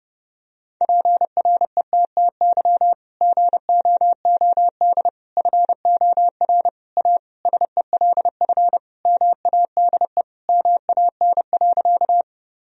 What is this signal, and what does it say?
Morse code, heard as a keyed tone, about 30 wpm, 700 Hz. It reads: PRETTY GOOD FOR A SELF MADE MAN.
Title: Sentences from Top 1000 Words - Review Part 1 - 30wpm